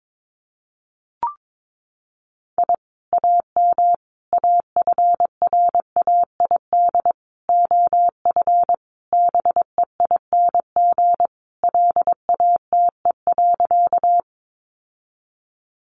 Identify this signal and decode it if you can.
Morse code, heard as a keyed tone, about 22 wpm, 700 Hz. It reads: I AM AFRAID OF BEING LATE.
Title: Sentences from Top 1500 Words 22wpm